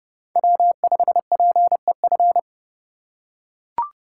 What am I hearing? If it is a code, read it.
Morse code, heard as a keyed tone, about 29 wpm, 700 Hz. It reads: W5PEF